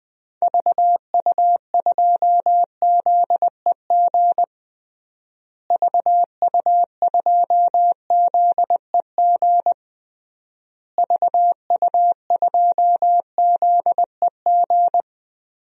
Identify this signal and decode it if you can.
Morse code, heard as a keyed tone, about 20 wpm, 700 Hz. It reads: VU2ZEG VU2ZEG VU2ZEG
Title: DXCC-100 Call Signs - 6 Characters - Repeated 3x - 20wpm